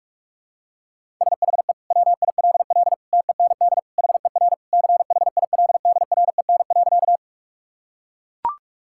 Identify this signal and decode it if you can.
Morse code, heard as a keyed tone, about 41 wpm, 700 Hz. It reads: SHE WILL TEND HER CHILDREN.
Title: Sentences from Top 1000 Words - Review Part 5 - 45wpm